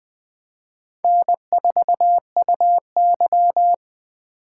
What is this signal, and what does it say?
Morse code, heard as a keyed tone, about 20 wpm, 700 Hz. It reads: N4UY
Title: Speed Racing - USA 1x2 Call Signs 17wpm